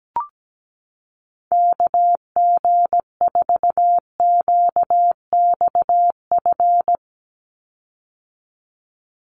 Morse code, heard as a keyed tone, about 17 wpm, 700 Hz. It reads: KG4QXF
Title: USA Call Signs 17wpm